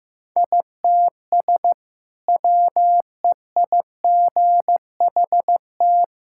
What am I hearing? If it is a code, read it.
Morse code, heard as a keyed tone, about 15 wpm, 700 Hz. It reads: ITS WEIGHT
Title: Sentences from Top 1000 Words - Review Part 6 - 15wpm